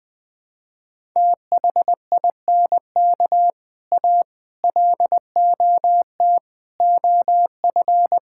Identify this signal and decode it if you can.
Morse code, heard as a keyed tone, about 21 wpm, 700 Hz. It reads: THINK A LOT OF
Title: Build A Phrase - 5-Word Phrases - 20wpm